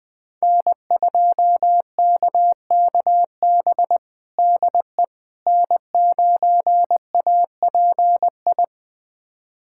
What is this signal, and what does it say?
Morse code, heard as a keyed tone, about 20 wpm, 700 Hz. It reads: N2KKB DE N9API
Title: USA Callsign DE Callsign - 5 Characters - 20wpm